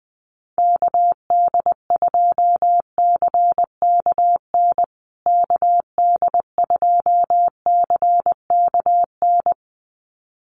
Morse code, heard as a keyed tone, about 20 wpm, 700 Hz. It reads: KD2CKN KD2CKN
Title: USA Call Signs Repeated 2x 20wpm